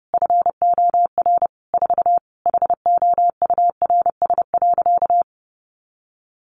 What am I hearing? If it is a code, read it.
Morse code, heard as a keyed tone, about 30 wpm, 700 Hz. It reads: FOR 4 HOURS.
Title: Additional Easy Phrases 30wpm